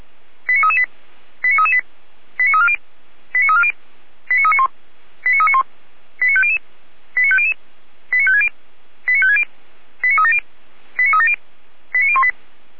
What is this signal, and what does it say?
Five-tone selective call (ZVEI), heard as quick runs of five tones, tones of about 80 ms. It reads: 89338 89338 89340 89340 89391 89391 89500 89500 89560 89560 89370 89370 89018
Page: ZVEI